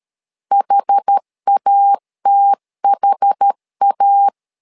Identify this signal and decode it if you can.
Morse code, heard as a keyed tone, about 13 wpm, 800 Hz. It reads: HATHA
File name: LEZIONI / 2 UHDN / 2 UHD 12wpm 1540